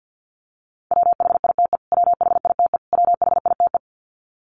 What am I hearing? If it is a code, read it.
Morse code, heard as a keyed tone, about 50 wpm, 700 Hz. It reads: WHITE WHITE WHITE
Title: Top 300 Words Repeated 50wpm